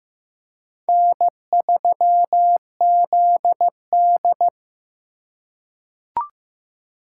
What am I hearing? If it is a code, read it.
Morse code, heard as a keyed tone, about 15 wpm, 700 Hz. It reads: N3ZD
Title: USA Call Signs - 4 Characters - 4x Character Spacing - 15wpm